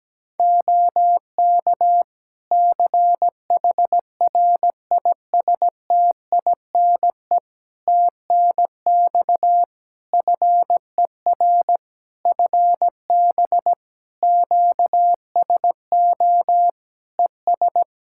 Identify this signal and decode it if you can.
Morse code, heard as a keyed tone, about 17 wpm, 700 Hz. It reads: OK CHRISTINE TNX FER FB QSO ES
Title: Practice QSO - Exchange 4 - Ending - 17wpm